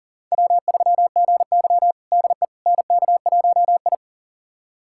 Morse code, heard as a keyed tone, about 40 wpm, 700 Hz. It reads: W3GY DE NK1I
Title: USA Callsign DE Callsign - 4 Characters - 40wpm